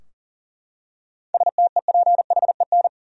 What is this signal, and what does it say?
Morse code, heard as a keyed tone, about 40 wpm, 700 Hz. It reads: STEPHEN
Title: USA Common Names - 40wpm